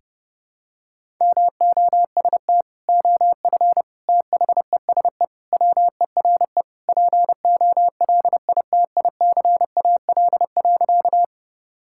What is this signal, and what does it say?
Morse code, heard as a keyed tone, about 30 wpm, 700 Hz. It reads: MOST OF THESE WERE POLITICAL.
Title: Sentences from Top 1000 Words - Review Part 5 - 30wpm